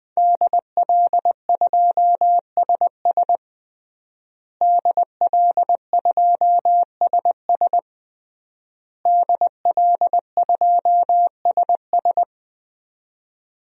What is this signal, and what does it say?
Morse code, heard as a keyed tone, about 20 wpm, 700 Hz. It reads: DL2SS DL2SS DL2SS
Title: DXCC-100 Call Signs - 5 Characters - Repeated 3x - 20wpm